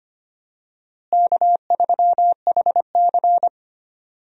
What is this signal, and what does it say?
Morse code, heard as a keyed tone, about 25 wpm, 700 Hz. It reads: K3HC
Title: USA Call Signs - 4 Characters - Repeated 3x - 25wpm